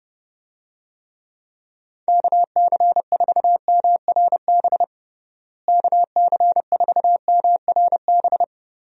Morse code, heard as a keyed tone, about 30 wpm, 700 Hz. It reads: KC4MRB KC4MRB
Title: USA Call Signs - 6 Characters - Repeated 2x - 30wpm